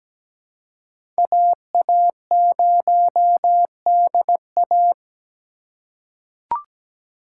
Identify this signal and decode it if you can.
Morse code, heard as a keyed tone, about 17 wpm, 700 Hz. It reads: AA0DA
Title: USA Call Signs - 4x Character Spacing - 17wpm